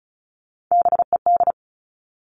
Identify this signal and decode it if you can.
Morse code, heard as a keyed tone, about 34 wpm, 700 Hz. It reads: BED
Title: All Cut Numbers - 3-Digit 35wpm